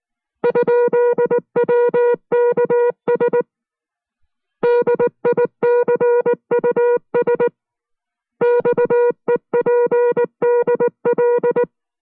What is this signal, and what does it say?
Morse code, heard as a keyed tone, about 19 wpm, 480 Hz. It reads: ?WKS DICUS XEPDL